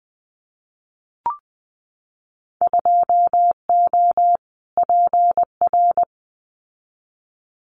Morse code, heard as a keyed tone, about 20 wpm, 700 Hz. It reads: 2O PR
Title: Winter Field Day Exchanges Encore 20wpm